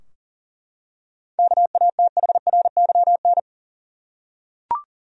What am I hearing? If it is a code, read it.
Morse code, heard as a keyed tone, about 40 wpm, 700 Hz. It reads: KATHRYN
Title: USA Common Names - 40wpm